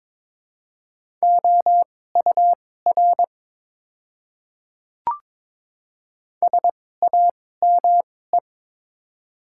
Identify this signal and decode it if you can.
Morse code, heard as a keyed tone, about 22 wpm, spaced as for 16 wpm, 700 Hz. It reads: OUR SAME
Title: 2x Character Spacing - Top 200 Words - 22wpm